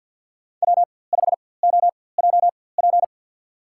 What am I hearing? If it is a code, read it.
Morse code, heard as a keyed tone, about 50 wpm, spaced as for 23 wpm, 700 Hz. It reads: W5OJP
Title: USA Call Signs - 5 Characters - 4x Character Spacing - 50wpm